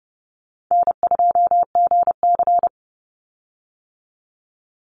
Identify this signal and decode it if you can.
Morse code, heard as a keyed tone, about 32 wpm, 700 Hz. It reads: N2GC
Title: Easy Call Signs - 30wpm